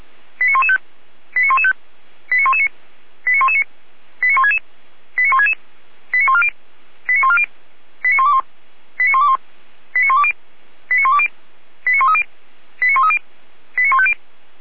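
Five-tone selective call (ZVEI), heard as quick runs of five tones, tones of about 70 ms. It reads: 89115 89115 89119 89119 89166 89166 89140 89140 89121 89121 89120 89120 89130 89130 89150
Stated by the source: ZVEI five-tone call 89080 received by SDR.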